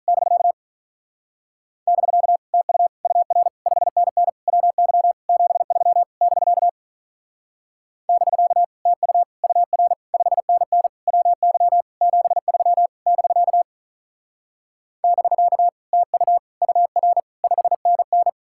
Morse code, heard as a keyed tone, about 40 wpm, 700 Hz. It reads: <BK> <BK> TU UR 5NN WY 73 <BK> <BK> TU UR 5NN WY 73 <BK> <BK> TU UR 5NN
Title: Speed-Racing Common QSO Phrases - 35wpm